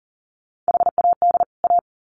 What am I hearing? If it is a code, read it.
Morse code, heard as a keyed tone, about 40 wpm, 700 Hz. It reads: HAD A
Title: Sets of 2 Words 40wpm